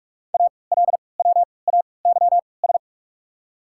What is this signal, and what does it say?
Morse code, heard as a keyed tone, about 46 wpm, spaced as for 26 wpm, 690 Hz. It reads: ALWAYS
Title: 3x Character Spacing - Top 300 Words - 45wpm